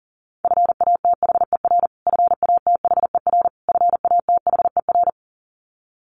Morse code, heard as a keyed tone, about 40 wpm, 700 Hz. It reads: FATHER FATHER FATHER
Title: Top 500 Words Repeated 40wpm